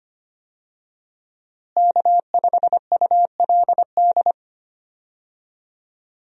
Morse code, heard as a keyed tone, about 25 wpm, 700 Hz. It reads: K5ULD
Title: USA Call Signs - 5 Characters - Repeated 2x - 25wpm